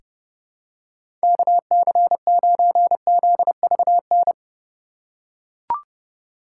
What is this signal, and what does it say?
Morse code, heard as a keyed tone, about 30 wpm, 700 Hz. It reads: KC9ZVN
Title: USA Call Signs - 6 Characters - 2x Character Spacing - 30wpm